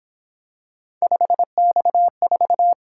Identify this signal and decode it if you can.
Morse code, heard as a keyed tone, about 26 wpm, 700 Hz. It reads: 5X4
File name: Speed-Racing - Three Letter-Number Combinations - 22wpm